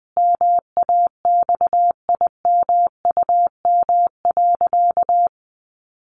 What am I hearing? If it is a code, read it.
Morse code, heard as a keyed tone, about 20 wpm, 700 Hz. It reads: MAXIMUM.
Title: Sentences from Top 1300 Words 20wpm